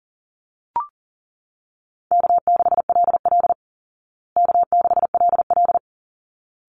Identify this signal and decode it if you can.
Morse code, heard as a keyed tone, about 40 wpm, 700 Hz. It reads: K6LL K6LL
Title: Call Signs Repeated 3x - 40wpm